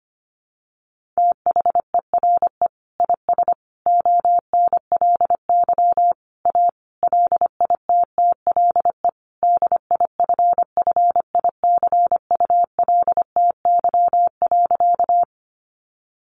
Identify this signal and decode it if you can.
Morse code, heard as a keyed tone, about 25 wpm, 700 Hz. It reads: THERE IS ONLY A LITTLE DIFFICULTY.
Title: Sentences from Top 1500 Words 25wpm